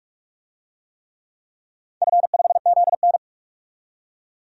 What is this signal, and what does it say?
Morse code, heard as a keyed tone, about 43 wpm, 700 Hz. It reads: F5ZN
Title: DXCC-100 Call Signs - 4 Characters - 45wpm